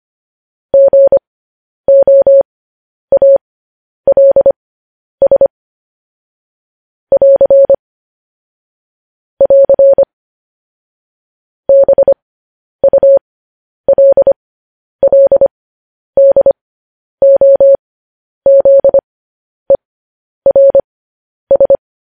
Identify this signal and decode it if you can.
Morse code, heard as a keyed tone, about 25 wpm, spaced as for 10 wpm, 550 Hz. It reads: GOALS <AR> <AR> BULLDOZERS
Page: news headlines in morse code at 10 wpm